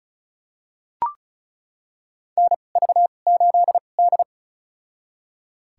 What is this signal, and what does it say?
Morse code, heard as a keyed tone, about 36 wpm, spaced as for 25 wpm, 700 Hz. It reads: NV8D
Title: USA Call Signs - 4 Characters - 2x Character Spacing - 35wpm